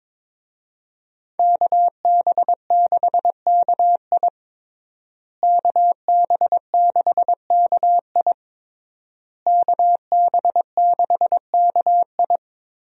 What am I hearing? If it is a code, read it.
Morse code, heard as a keyed tone, about 22 wpm, 690 Hz. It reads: KB6KI KB6KI KB6KI
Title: USA Call Signs - 5 Characters - Repeated 3x - 22wpm